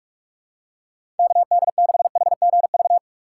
Morse code, heard as a keyed tone, about 46 wpm, 700 Hz. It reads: KD6HGV
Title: USA Call Signs - 6 Characters - 6x Character Spacing - 45wpm